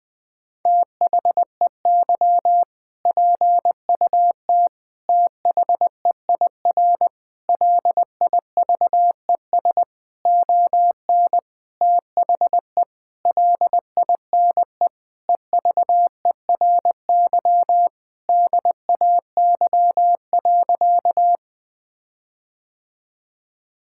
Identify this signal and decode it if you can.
Morse code, heard as a keyed tone, about 20 wpm, 700 Hz. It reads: THEY PUT THEIR LIVES ON THE LINE EVERY DAY.
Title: Sentences with 8-Word Phrases - 20wpm